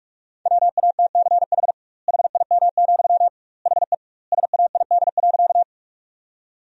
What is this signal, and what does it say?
Morse code, heard as a keyed tone, about 48 wpm, 700 Hz. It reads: WATCH HIM, HE SAID.
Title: 2x Word-Spacing - Sentences from Top 300 Words - 45wpm